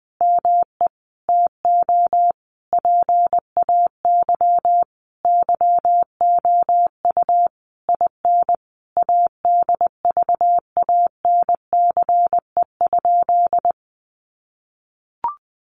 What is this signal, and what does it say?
Morse code, heard as a keyed tone, about 20 wpm, 700 Hz. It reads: ME TO PAY YOU IN ADVANCE?
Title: Sentences from Top 1400 Words 20wpm